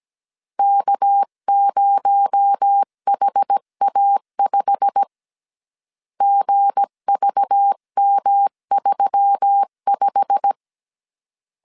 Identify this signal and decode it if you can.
Morse code, heard as a keyed tone, about 17 wpm, 800 Hz. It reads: K0HA5 GVM35
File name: CW / DET / 1 9 1500C S B 16wpm 2856